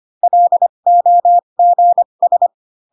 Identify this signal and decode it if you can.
Morse code, heard as a keyed tone, about 25 wpm, spaced as for 22 wpm, 700 Hz. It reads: LOGS
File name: Pinocchio / Pinocchio ch3 18 25 ik1hge